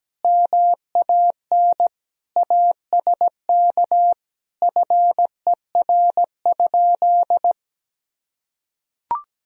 Morse code, heard as a keyed tone, about 17 wpm, 700 Hz. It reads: MAN ASK FER?
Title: Sentences from Top 200 Words and Abbreviations 17wpm